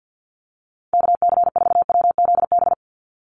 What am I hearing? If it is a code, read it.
Morse code, heard as a keyed tone, about 50 wpm, 700 Hz. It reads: KC4WZB
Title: NA SOTA Activator Callsigns 50wpm